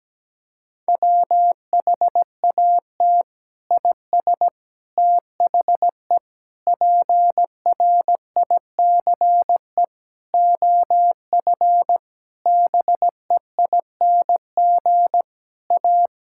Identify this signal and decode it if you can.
Morse code, heard as a keyed tone, about 17 wpm, 700 Hz. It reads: WHAT IS THE PRICE OF BEING A
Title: Sentences from Top 1000 Words - Review Part 5 - 17wpm